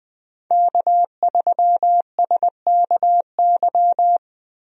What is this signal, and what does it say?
Morse code, heard as a keyed tone, about 20 wpm, 700 Hz. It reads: K3SKY
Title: USA Call Signs - 5 Characters - Repeated 2x - 20wpm